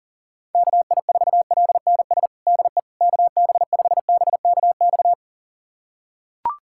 Morse code, heard as a keyed tone, about 40 wpm, 700 Hz. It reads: KI4LNS DE KB5BKX
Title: USA Callsign DE Callsign 40wpm